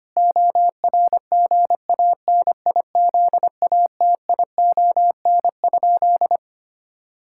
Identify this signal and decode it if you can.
Morse code, heard as a keyed tone, about 25 wpm, 700 Hz. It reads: ORGANIZATION?
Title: Sentences from Top 1000 Words - Review Part 5 - 25wpm